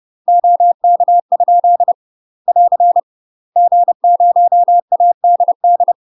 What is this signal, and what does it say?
Morse code, heard as a keyed tone, about 30 wpm, 700 Hz. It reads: OK? <AR> G0ADD
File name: UK MASTER QSOs TRAINING / 30WPM / UK MASTER QSOs 0043